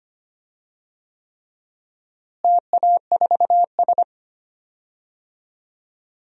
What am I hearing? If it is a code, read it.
Morse code, heard as a keyed tone, about 25 wpm, 700 Hz. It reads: TA4S
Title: DXCC-100 Call Signs - 4 Characters - Repeated 3x - 25wpm